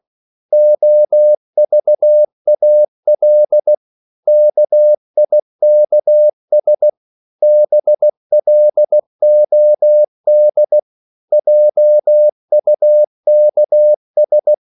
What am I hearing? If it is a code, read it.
Morse code, heard as a keyed tone, about 16 wpm, 600 Hz. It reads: OVAL KIKS BLOD JUKS
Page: Korte danske ord 16wpm